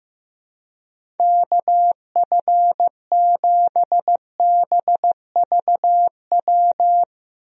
Morse code, heard as a keyed tone, about 15 wpm, 700 Hz. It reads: KF7BVW
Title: USA Call Signs - 6 Characters - Repeated 3x - 15wpm